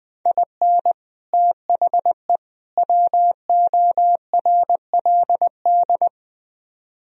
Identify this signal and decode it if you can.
Morse code, heard as a keyed tone, about 20 wpm, 700 Hz. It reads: IN THE WORLD
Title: Speed-Racing - Sets of 3 Words 17wpm